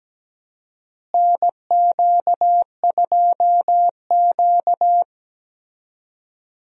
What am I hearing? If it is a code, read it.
Morse code, heard as a keyed tone, about 17 wpm, 700 Hz. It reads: NQ2Q
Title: USA Call Signs - 4 Characters - Repeated 3x - 17wpm